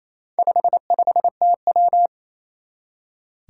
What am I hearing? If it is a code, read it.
Morse code, heard as a keyed tone, about 27 wpm, 700 Hz. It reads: 55TW
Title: Four Challenging Letter-Number Combinations 28wpm